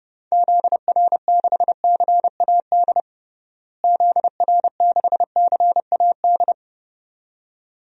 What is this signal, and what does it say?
Morse code, heard as a keyed tone, about 31 wpm, 700 Hz. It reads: ZR6CAD ZR6CAD
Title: DXCC-100 Call Signs - 6 Characters - Repeated 2x - 30wpm